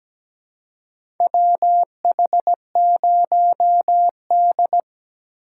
Morse code, heard as a keyed tone, about 17 wpm, 700 Hz. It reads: WH0D
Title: USA Call Signs - 4 Characters - 4x Character Spacing - 17wpm